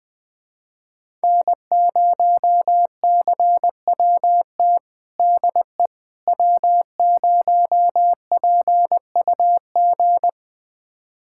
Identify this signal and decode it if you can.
Morse code, heard as a keyed tone, about 20 wpm, 700 Hz. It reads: N0CWT DE W0PUG
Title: USA Callsign DE Callsign - 5 Characters - 20wpm